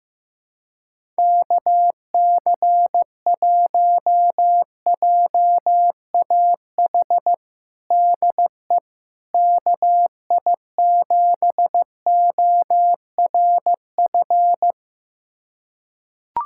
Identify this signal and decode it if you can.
Morse code, heard as a keyed tone, about 15 wpm, 700 Hz. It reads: KC1JAH DE KI7ORF
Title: USA Callsign DE Callsign 15wpm